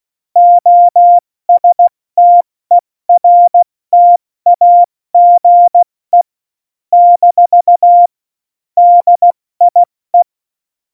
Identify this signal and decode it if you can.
Morse code, heard as a keyed tone, about 16 wpm, 700 Hz. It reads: OSTERTAGE - DIE
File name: Morsekurs / Witze / Witz 28 April 2020, 70 F80